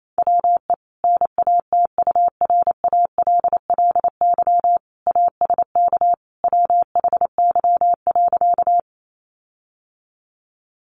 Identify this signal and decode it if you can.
Morse code, heard as a keyed tone, about 28 wpm, 700 Hz. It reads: WE NATURALLY ASK WHY.
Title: Sentences from Top 1300 Words 28wpm